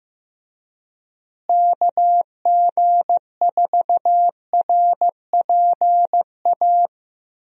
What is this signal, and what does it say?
Morse code, heard as a keyed tone, about 15 wpm, 700 Hz. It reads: KG4RPA
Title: USA Call Signs - 6 Characters - Repeated 3x - 15wpm